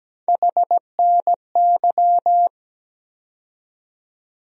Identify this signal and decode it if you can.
Morse code, heard as a keyed tone, about 17 wpm, 700 Hz. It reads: HNY
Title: USA Call Sign Postfixes - 3 Letters 17wpm